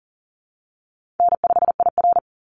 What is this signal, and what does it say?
Morse code, heard as a keyed tone, about 40 wpm, 700 Hz. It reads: N5IR
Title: Call Signs - 40wpm